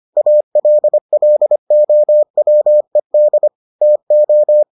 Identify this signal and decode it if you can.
Morse code, heard as a keyed tone, about 25 wpm, 600 Hz. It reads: ALLOWED TO